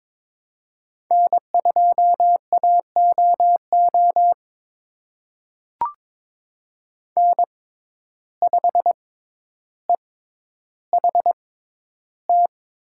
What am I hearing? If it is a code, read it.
Morse code, heard as a keyed tone, about 22 wpm, spaced as for 11 wpm, 700 Hz. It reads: N2AOO N5EHT
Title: USA Call Signs - 5 Characters - 6x Character Spacing - 22wpm